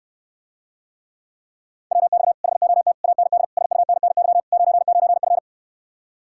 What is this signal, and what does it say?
Morse code, heard as a keyed tone, about 69 wpm, 700 Hz. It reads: WX HOT AND SUNNY 99F